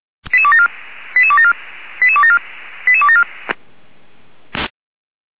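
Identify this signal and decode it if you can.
Five-tone selective call (ZVEI), heard as quick runs of five tones, tones of about 70 ms. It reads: 88274 88274 88274 88274
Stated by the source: ZVEI five-tone selective call 88274 received by SDR.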